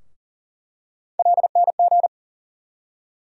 Morse code, heard as a keyed tone, about 40 wpm, 700 Hz. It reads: LNG
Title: USA Call Sign Postfixes - 3 Letters 40wpm